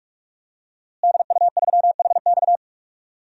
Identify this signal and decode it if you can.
Morse code, heard as a keyed tone, about 46 wpm, 700 Hz. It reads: DU3HX
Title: DXCC-100 Call Signs - 5 Characters - Repeated 2x - 45wpm